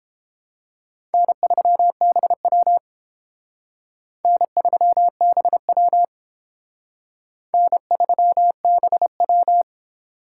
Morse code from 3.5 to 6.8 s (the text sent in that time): N3BW